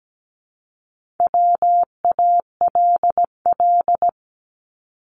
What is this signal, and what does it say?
Morse code, heard as a keyed tone, about 17 wpm, 700 Hz. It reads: WALL
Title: Multi-letter Phonemes Repeated 3x 17wpm